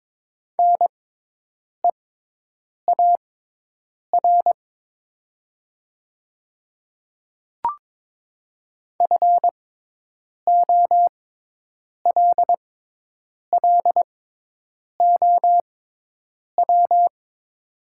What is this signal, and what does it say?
Morse code, heard as a keyed tone, about 22 wpm, spaced as for 8 wpm, 690 Hz. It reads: NEAR FOLLOW
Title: 6x Character Spacing - Top 200 Words - 22wpm